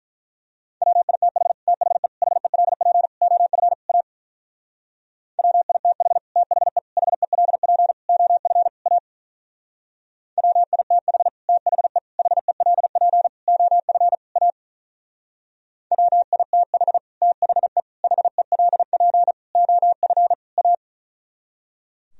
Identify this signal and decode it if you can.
Morse code, heard as a keyed tone, about 42 wpm, 700 Hz. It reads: WITH THE HELP OF A WITH THE HELP OF A WITH THE HELP OF A WITH THE HELP OF A
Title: Speed-Racing - Sets of 5 Words 35wpm